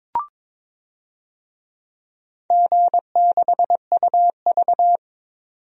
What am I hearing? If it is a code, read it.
Morse code, heard as a keyed tone, about 21 wpm, 700 Hz. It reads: G6UV
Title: DXCC-100 Call Signs - 4 Characters - 22wpm